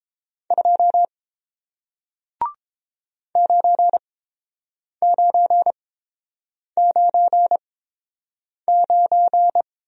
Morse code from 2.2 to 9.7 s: 9999

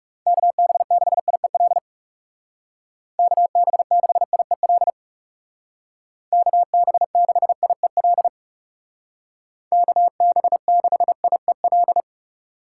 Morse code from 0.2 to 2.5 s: KB6IEL

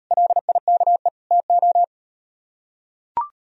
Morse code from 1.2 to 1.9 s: TO

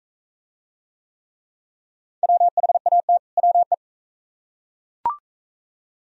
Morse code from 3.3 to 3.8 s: WE

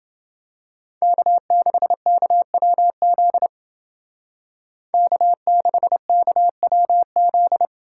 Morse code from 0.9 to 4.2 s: K6KWZ

